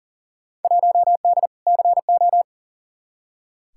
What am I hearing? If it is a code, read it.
Morse code, heard as a keyed tone, about 37 wpm, 700 Hz. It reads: 1D CO
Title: Field Day Exchanges - 40wpm